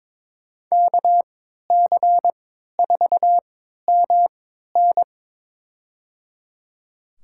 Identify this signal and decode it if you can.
Morse code, heard as a keyed tone, about 21 wpm, spaced as for 12 wpm, 700 Hz. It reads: KC4MN
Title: USA Call Signs - 5 Characters - 3x Character Spacing - 22wpm